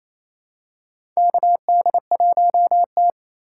Morse code, heard as a keyed tone, about 28 wpm, 700 Hz. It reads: KD1T
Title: USA Call Signs - 4 Characters - 4x Character Spacing - 28wpm